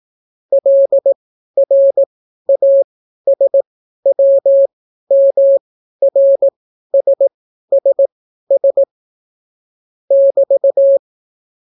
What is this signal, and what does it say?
Morse code, heard as a keyed tone, about 18 wpm, spaced as for 12 wpm, 550 Hz. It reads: LRASWMRSSS <BT>